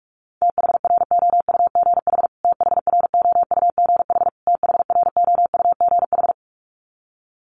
Morse code, heard as a keyed tone, about 44 wpm, 700 Hz. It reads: THROUGH THROUGH THROUGH
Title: Top 300 Words Repeated 45wpm